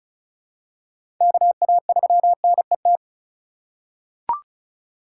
Morse code, heard as a keyed tone, about 34 wpm, 700 Hz. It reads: KA3NET
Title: USA Call Signs - 6 Characters - 4x Character Spacing - 35wpm